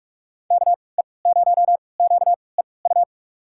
Morse code, heard as a keyed tone, about 46 wpm, spaced as for 25 wpm, 690 Hz. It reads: KE0QEU
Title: USA Call Signs - 6 Characters - 3x Character Spacing - 45wpm